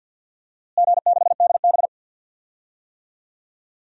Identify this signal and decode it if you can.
Morse code, heard as a keyed tone, about 47 wpm, 700 Hz. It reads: G6DB
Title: DXCC-100 Call Signs - 4 Characters - Repeated 2x - 50wpm